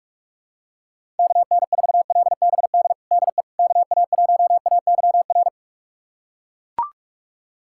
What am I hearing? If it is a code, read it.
Morse code, heard as a keyed tone, about 41 wpm, 700 Hz. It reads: KN4LBD DE KA1AYR
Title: USA Callsign DE Callsign - 6 Characters - 45wpm